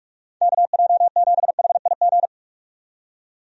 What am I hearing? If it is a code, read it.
Morse code, heard as a keyed tone, about 46 wpm, 700 Hz. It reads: KJ7HIG
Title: USA Call Signs - 6 Characters - 45wpm